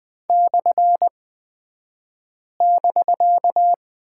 Morse code, heard as a keyed tone, about 20 wpm, spaced as for 5 wpm, 700 Hz. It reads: /<BK>
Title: 249 Lesson 40d - BK - Two Characters 20wpm